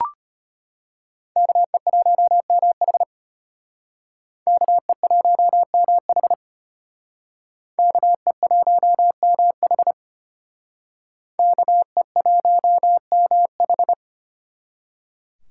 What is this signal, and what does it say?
Morse code, heard as a keyed tone, about 32 wpm, 700 Hz. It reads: KE1MH KE1MH KE1MH KE1MH